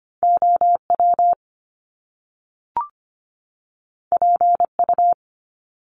Morse code, heard as a keyed tone, about 25 wpm, 700 Hz. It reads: O W P U
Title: Top Two Letter Combinations in Words 25wpm